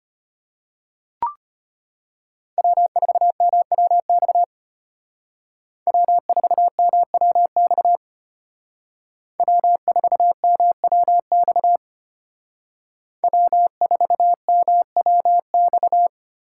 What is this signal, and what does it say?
Morse code, heard as a keyed tone, about 33 wpm, 700 Hz. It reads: W4MWX W4MWX W4MWX W4MWX